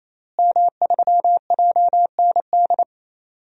Morse code, heard as a keyed tone, about 28 wpm, 700 Hz. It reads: M3JND
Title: DXCC-100 Call Signs - 5 Characters - Repeated 3x - 28wpm